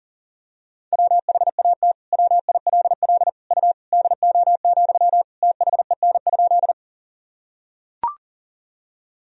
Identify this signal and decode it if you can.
Morse code, heard as a keyed tone, about 40 wpm, 700 Hz. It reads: WHAT WILL U DO, THEN?
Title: Sentences from Top 100 Words and Abbreviations 40wpm